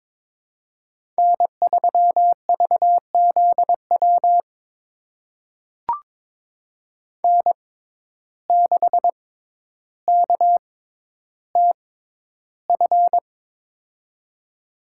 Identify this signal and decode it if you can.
Morse code, heard as a keyed tone, about 22 wpm, spaced as for 11 wpm, 700 Hz. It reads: N3VZW N6KTF
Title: USA Call Signs - 5 Characters - 6x Character Spacing - 22wpm